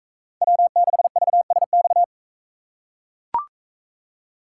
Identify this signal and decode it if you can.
Morse code, heard as a keyed tone, about 42 wpm, 700 Hz. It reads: W6VSX